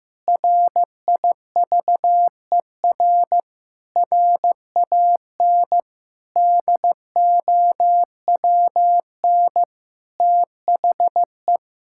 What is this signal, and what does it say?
Morse code, heard as a keyed tone, about 15 wpm, 700 Hz. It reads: RIVER RAN DOWN THE
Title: Sentences from Top 1000 Words - Review Part 6 - 15wpm